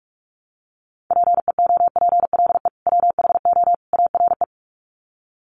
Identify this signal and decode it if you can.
Morse code, heard as a keyed tone, about 44 wpm, 700 Hz. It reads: PEOPLE WHO ARE
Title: Sets of 3 Words 45wpm